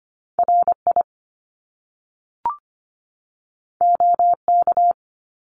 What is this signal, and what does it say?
Morse code, heard as a keyed tone, about 25 wpm, 700 Hz. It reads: RI OK
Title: Top Two Letter Combinations in Words 25wpm